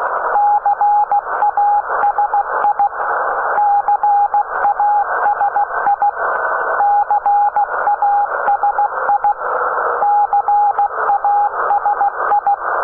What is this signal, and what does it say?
Morse code, heard as a keyed tone, about 16 wpm, 820 Hz, with radio noise. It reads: CASI CASI CASI CASI
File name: CAS-2T 21h00 utc